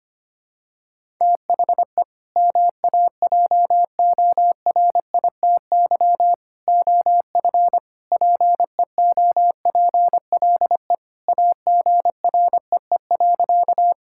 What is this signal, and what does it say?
Morse code, heard as a keyed tone, about 25 wpm, 700 Hz. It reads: THE MAJORITY OF PEOPLE AGREE.
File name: practice / 25wpm / Sentences with 3-Word Phrases - 25wpm